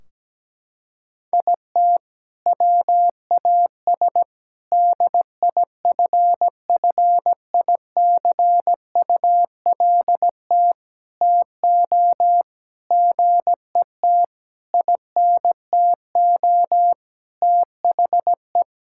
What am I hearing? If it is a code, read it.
Morse code, heard as a keyed tone, about 17 wpm, 700 Hz. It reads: IT WAS DIFFICULT TO GET INTO THE
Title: Sentences from Top 1600 Words 17wpm